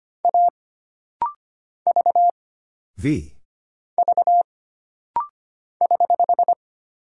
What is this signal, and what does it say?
Morse code, heard as a keyed tone, about 25 wpm, 700 Hz. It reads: A V V <HH>
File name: Single Character (Letter:Number:Punctuation:Prosign) - Rapid-Fire - 25wpm